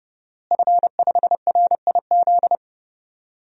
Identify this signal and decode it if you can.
Morse code, heard as a keyed tone, about 32 wpm, 700 Hz. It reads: F5RIZ
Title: DXCC-100 Call Signs - 5 Characters - Repeated 3x - 30wpm